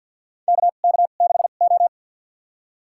Morse code, heard as a keyed tone, about 51 wpm, spaced as for 36 wpm, 700 Hz. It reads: KK6O
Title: USA Call Signs - 4 Characters - 2x Character Spacing - 50wpm